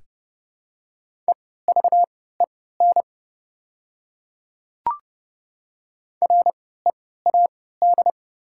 Morse code, heard as a keyed tone, about 29 wpm, spaced as for 17 wpm, 700 Hz. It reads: EVEN READ
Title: 3x Character Spacing - Top 200 Words - 30wpm